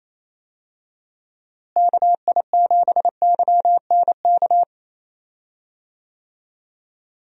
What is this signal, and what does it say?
Morse code, heard as a keyed tone, about 28 wpm, 700 Hz. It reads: KI7YNK
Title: USA Call Signs - 6 Characters - Repeated 3x - 28wpm